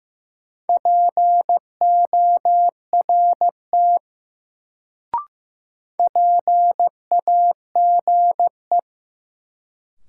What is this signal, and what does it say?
Morse code, heard as a keyed tone, about 15 wpm, 700 Hz. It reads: PORT PAGE
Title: Sentences from Top 1000 Words - Review Part 1 - New Word Drill - 15wpm